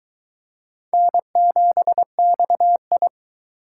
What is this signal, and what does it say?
Morse code, heard as a keyed tone, about 22 wpm, 700 Hz. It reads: N7XI